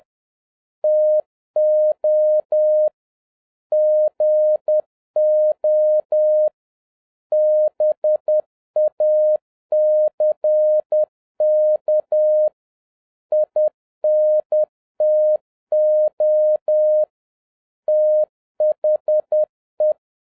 Morse code, heard as a keyed tone, about 10 wpm, 610 Hz. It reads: TO GO BACK INTO THE